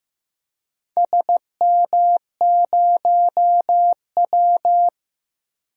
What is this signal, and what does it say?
Morse code, heard as a keyed tone, about 15 wpm, 700 Hz. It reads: SM0W
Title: DXCC-100 Call Signs - 4 Characters - Repeated 3x - 15wpm